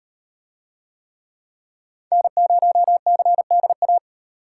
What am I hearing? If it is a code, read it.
Morse code, heard as a keyed tone, about 37 wpm, 700 Hz. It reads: N0CDA